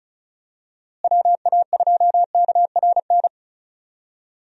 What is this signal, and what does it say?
Morse code, heard as a keyed tone, about 34 wpm, 700 Hz. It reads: WA2KRN
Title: USA Call Signs - 6 Characters - Repeated 2x - 35wpm